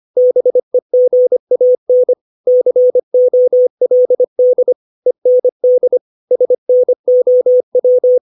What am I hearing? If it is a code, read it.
Morse code, heard as a keyed tone, about 25 wpm, 500 Hz. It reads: BEGAN COLD END SNOW
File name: 632 words / 632 Words 25WPM